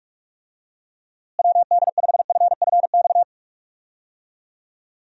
Morse code, heard as a keyed tone, about 46 wpm, 700 Hz. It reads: WD5FFX
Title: USA Call Signs - 6 Characters - Repeated 3x - 45wpm